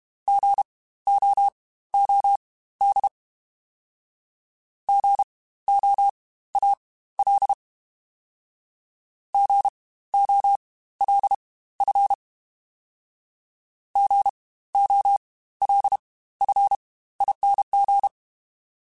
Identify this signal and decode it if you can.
Morse code, heard as a keyed tone, about 32 wpm, spaced as for 16 wpm, 800 Hz. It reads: GOOD GOAL GOLF GOLFING
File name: Spaced Alphas / Spaced Alphas mp3s / Gs sp @30